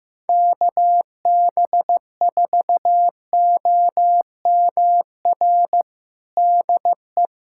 Morse code, heard as a keyed tone, about 15 wpm, 700 Hz. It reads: KB4OMR DE
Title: USA Callsign DE Callsign - 6 Characters - 15wpm